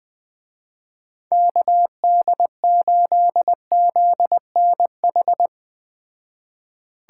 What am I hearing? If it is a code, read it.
Morse code, heard as a keyed tone, about 21 wpm, 700 Hz. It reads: KD8ZNH
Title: Speed Racing - USA 2x3 Call Signs 17wpm